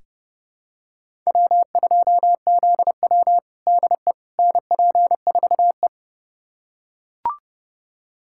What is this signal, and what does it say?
Morse code, heard as a keyed tone, about 32 wpm, 700 Hz. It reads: W2ZW DE NP4E
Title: USA Callsign DE Callsign - 4 Characters - 30wpm